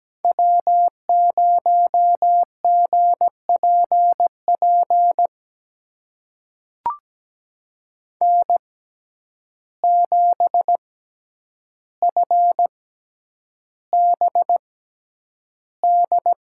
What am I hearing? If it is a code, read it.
Morse code, heard as a keyed tone, about 17 wpm, spaced as for 9 wpm, 700 Hz. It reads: W0GPP N7FBD
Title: USA Call Signs - 5 Characters - 6x Character Spacing - 17wpm